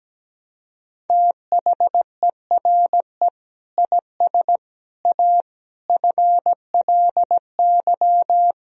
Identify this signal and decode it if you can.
Morse code, heard as a keyed tone, about 17 wpm, 700 Hz. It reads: THERE IS A FLY
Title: Sentences from Top 1000 Words - Review Part 2 - 17wpm